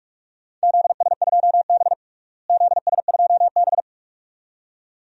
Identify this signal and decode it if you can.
Morse code, heard as a keyed tone, about 42 wpm, 700 Hz. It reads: ZS2B ZS2B
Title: DXCC-100 Call Signs - 4 Characters - Repeated 2x - 45wpm